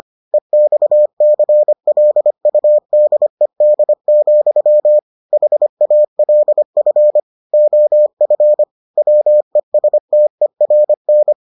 Morse code, heard as a keyed tone, about 25 wpm, 600 Hz. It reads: EXCLUDED, HALF OF WESTERN